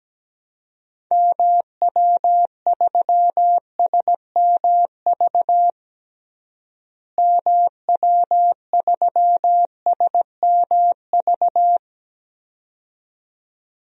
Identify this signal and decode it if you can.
Morse code, heard as a keyed tone, about 17 wpm, 700 Hz. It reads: MW3SMV MW3SMV
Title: DXCC-100 Call Signs - 6 Characters - Repeated 2x - 17wpm